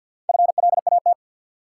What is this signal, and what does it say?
Morse code, heard as a keyed tone, about 50 wpm, 700 Hz. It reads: FLAT
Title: Sentences from Top 1000 Words - New Word Drill - 50wpm